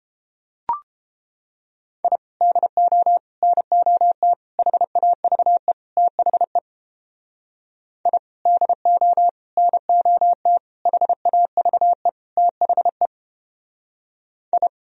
Morse code, heard as a keyed tone, about 31 wpm, 700 Hz. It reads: I DO NOT HAVE THE I DO NOT HAVE THE I